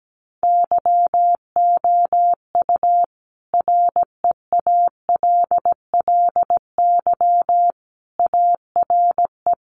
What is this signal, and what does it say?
Morse code, heard as a keyed tone, about 17 wpm, 700 Hz. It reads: YOU REALLY ARE
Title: Sentences from Top 1200 Words 17wpm